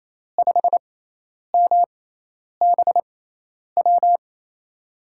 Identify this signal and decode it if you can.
Morse code, heard as a keyed tone, about 28 wpm, 700 Hz. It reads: 5 M B W
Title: USA Call Signs - 5 Characters - 6x Character Spacing - 28wpm